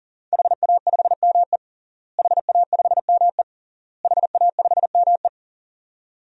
Morse code, heard as a keyed tone, about 40 wpm, 700 Hz. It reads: HA5ME HA5ME HA5ME
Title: DXCC-100 Call Signs - 5 Characters - Repeated 3x - 40wpm